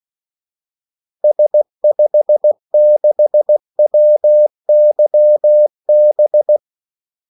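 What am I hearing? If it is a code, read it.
Morse code, heard as a keyed tone, about 16 wpm, 600 Hz. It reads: S56WYB
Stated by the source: Morse call sign practice at 16 wpm.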